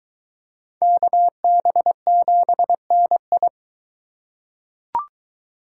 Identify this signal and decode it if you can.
Morse code, heard as a keyed tone, about 21 wpm, 700 Hz. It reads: KB7NI